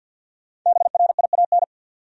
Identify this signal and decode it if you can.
Morse code, heard as a keyed tone, about 49 wpm, 700 Hz. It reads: BRIAN
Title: USA Common Names - 50wpm